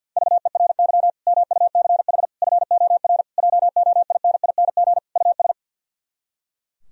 Morse code, heard as a keyed tone, about 64 wpm, 700 Hz. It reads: VERY MUCH FOR JOINING US